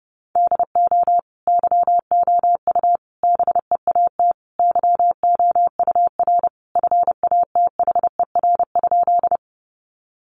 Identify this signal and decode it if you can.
Morse code, heard as a keyed tone, about 30 wpm, 700 Hz. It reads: DO YOU BEAT YOUR FATHER?
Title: Sentences from Top 1300 Words 30wpm